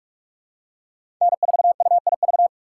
Morse code, heard as a keyed tone, about 46 wpm, 700 Hz. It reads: N4UIV
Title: USA Call Signs - 5 Characters - 2x Character Spacing - 45wpm